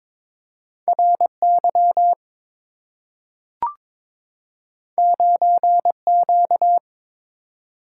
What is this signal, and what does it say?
Morse code, heard as a keyed tone, about 22 wpm, 700 Hz. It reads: RY 9Q
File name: practice / 22wpm / Two Letter-Number Combinations - 22wpm